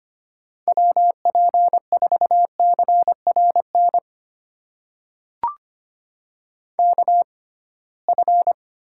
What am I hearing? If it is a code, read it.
Morse code, heard as a keyed tone, about 25 wpm, spaced as for 19 wpm, 700 Hz. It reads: WP4CRN KF
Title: USA Call Signs - 6 Characters - 6x Character Spacing - 25wpm